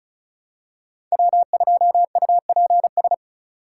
Morse code, heard as a keyed tone, about 36 wpm, 700 Hz. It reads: W2UPS
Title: USA Call Signs - 5 Characters - 4x Character Spacing - 35wpm